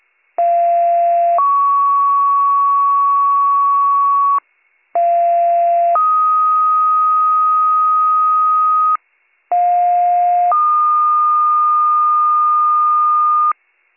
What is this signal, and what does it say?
Two-tone page, heard as a one-second tone, then a three-second tone, about 690 then 1120 Hz, about 690 then 1250 Hz, about 710 then 1220 Hz. It is heard from 0.4 to 13.5 s.